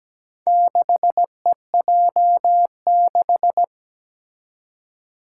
Morse code, heard as a keyed tone, about 17 wpm, 700 Hz. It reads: 6EJ6
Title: Four Challenging Letter-Number Combinations 17wpm